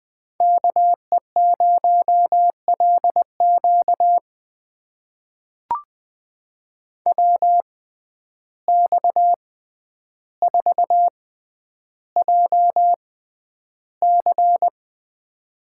Morse code, heard as a keyed tone, about 20 wpm, spaced as for 10 wpm, 700 Hz. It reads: KE0LQ WX4JC
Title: USA Call Signs - 6x Character Spacing - 20wpm